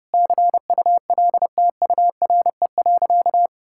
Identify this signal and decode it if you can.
Morse code, heard as a keyed tone, about 30 wpm, 700 Hz. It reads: CULTURE.
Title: Sentences from Top 1000 Words - Review Part 6 - 30wpm